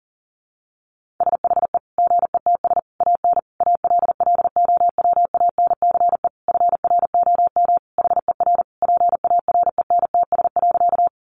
Morse code, heard as a keyed tone, about 40 wpm, 700 Hz. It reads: SHE GETS AN ALLOWANCE FROM HER PARENTS.
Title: Easy Phrases 40wpm